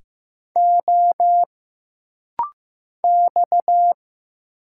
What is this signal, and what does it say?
Morse code, heard as a keyed tone, about 15 wpm, 700 Hz. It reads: O X
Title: Single Letters - Rapid-Fire 15wpm